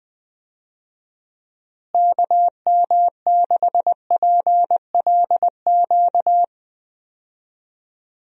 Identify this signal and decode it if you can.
Morse code, heard as a keyed tone, about 21 wpm, 700 Hz. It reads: KM6PLQ
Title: USA Call Signs - 6 Characters - Repeated 3x - 20wpm